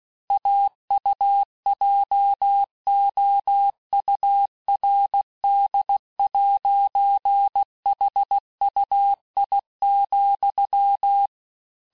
Morse code, heard as a keyed tone, about 16 wpm, 800 Hz. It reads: AUJOURD'HUI,
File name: cours-du-lundi files / cours-1592--teste-FIN-de-QSO-vitesse-15mpm